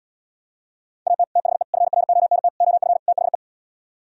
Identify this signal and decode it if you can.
Morse code, heard as a keyed tone, about 77 wpm, 700 Hz. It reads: AT THE FRONT OF THE